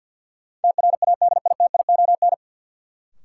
Morse code, heard as a keyed tone, about 52 wpm, 700 Hz. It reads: TRADITION
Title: Sentences from Top 1500 Words - New Word Drill - 50wpm